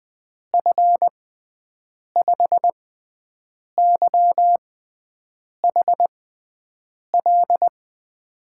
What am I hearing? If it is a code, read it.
Morse code, heard as a keyed tone, about 21 wpm, spaced as for 7 wpm, 700 Hz. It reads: F5YHL
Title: USA Call Signs - 6 Characters - 6x Character Spacing - 20wpm